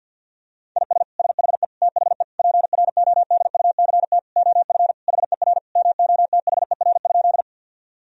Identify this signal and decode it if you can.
Morse code, heard as a keyed tone, about 50 wpm, 700 Hz. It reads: IS SHE THE PRODUCT OF HER MOTHER?
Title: Sentences from Top 1000 Words - Review Part 2 - 50wpm